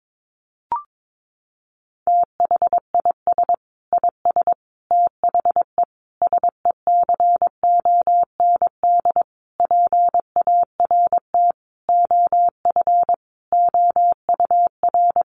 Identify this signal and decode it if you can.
Morse code, heard as a keyed tone, about 22 wpm, 700 Hz. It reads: THIS IS THE SECOND PART OF OUR
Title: Sentences from Top 1400 Words 22wpm